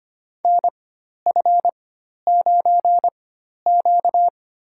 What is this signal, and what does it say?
Morse code, heard as a keyed tone, about 25 wpm, spaced as for 12 wpm, 700 Hz. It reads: NF9Q
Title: USA Call Signs - 4 Characters - 4x Character Spacing - 25wpm